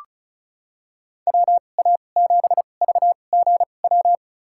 Morse code, heard as a keyed tone, about 36 wpm, spaced as for 25 wpm, 700 Hz. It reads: WA7VGW
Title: USA Call Signs - 6 Characters - 2x Character Spacing - 35wpm